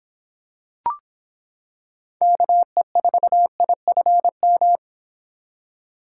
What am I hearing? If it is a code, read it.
Morse code, heard as a keyed tone, about 26 wpm, 700 Hz. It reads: KE4IFM